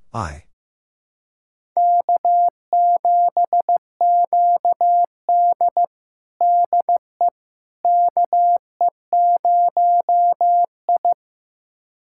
Morse code, heard as a keyed tone, about 15 wpm, 700 Hz. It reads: K7QD DE KE0I
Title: USA Callsign DE Callsign - 4 Characters - 15wpm